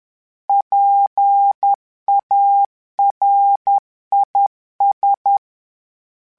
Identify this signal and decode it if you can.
Morse code, heard as a keyed tone, about 10 wpm, 800 Hz. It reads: PARIS
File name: PARIS Calibration / PARIS mp3s / PARIS@10